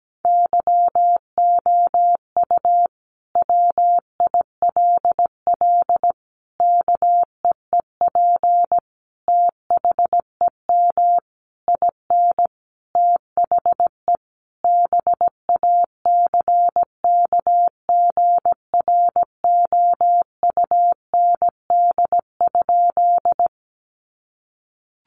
Morse code, heard as a keyed tone, about 17 wpm, 700 Hz. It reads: YOU WILL KEEP THEM IN THE BACKGROUND?
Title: Sentences from Top 1300 Words 17wpm